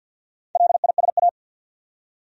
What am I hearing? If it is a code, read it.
Morse code, heard as a keyed tone, about 52 wpm, 700 Hz. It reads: LISA
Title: USA Common Names - 50wpm